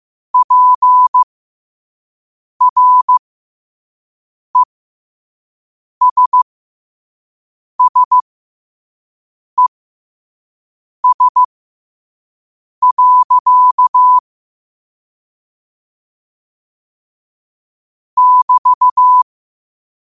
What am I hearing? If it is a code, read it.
Morse code, heard as a keyed tone, about 15 wpm, spaced as for 5 wpm, 1000 Hz. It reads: PRESSES. <BT>
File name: qotdfiles / 5wpm